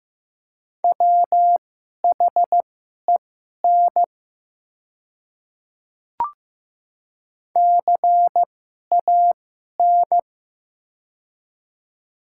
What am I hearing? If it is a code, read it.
Morse code, heard as a keyed tone, about 15 wpm, spaced as for 11 wpm, 700 Hz. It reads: WHEN CAN
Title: 2x Character Spacing - Top 100 Words - 15wpm